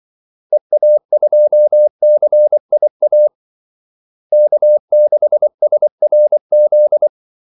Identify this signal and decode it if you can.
Morse code, heard as a keyed tone, about 24 wpm, 600 Hz. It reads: EA2CIA K6SRZ
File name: Callsigns-24wpm